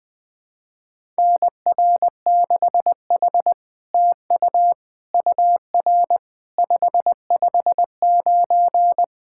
Morse code, heard as a keyed tone, about 20 wpm, 700 Hz. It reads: NR6H TU UR 559
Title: SOTA QSO Practice 20wpm